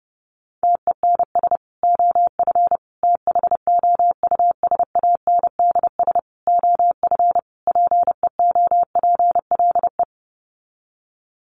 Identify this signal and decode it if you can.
Morse code, heard as a keyed tone, about 28 wpm, 700 Hz. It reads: TENS OF THOUSANDS OF PEOPLE
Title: Sets of 5 Words Encore2 30wpm